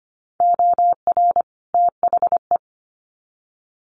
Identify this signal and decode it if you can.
Morse code, heard as a keyed tone, about 25 wpm, 700 Hz. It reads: OR THE
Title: Sets of 2 Words 25wpm